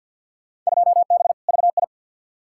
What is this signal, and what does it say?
Morse code, heard as a keyed tone, about 47 wpm, 700 Hz. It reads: 2B VI
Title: Field Day Exchanges Encore - 50wpm